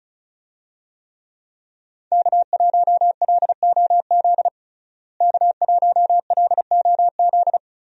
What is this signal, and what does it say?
Morse code, heard as a keyed tone, about 34 wpm, 700 Hz. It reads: K1LOZ K1LOZ
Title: USA Call Signs - 5 Characters - Repeated 3x - 35wpm